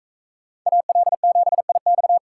Morse code, heard as a keyed tone, about 40 wpm, 700 Hz. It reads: AL7IX